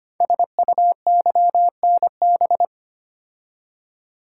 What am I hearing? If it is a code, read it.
Morse code, heard as a keyed tone, about 25 wpm, 700 Hz. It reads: SUYNB
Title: Code Groups - 5 Letters - 25wpm